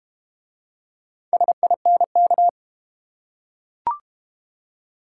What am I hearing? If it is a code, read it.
Morse code, heard as a keyed tone, about 32 wpm, 700 Hz. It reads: SINK